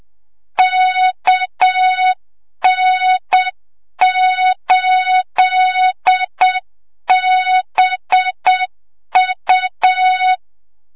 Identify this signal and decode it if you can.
Morse code, heard as a keyed tone, about 7 wpm, 3700 Hz, with noise. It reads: KN8BU
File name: Morze-HA2MN / hang / 25 GYAKORLAT